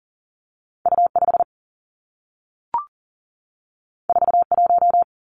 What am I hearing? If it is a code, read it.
Morse code, heard as a keyed tone, about 42 wpm, 700 Hz. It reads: U 5 4 1
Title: Two Challenging Letter-Number Combinations 40wpm